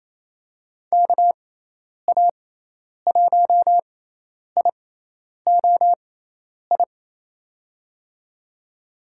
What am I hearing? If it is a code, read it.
Morse code, heard as a keyed tone, about 28 wpm, spaced as for 10 wpm, 700 Hz. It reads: KA1IOI
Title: USA Call Signs - 6 Characters - 6x Character Spacing - 28wpm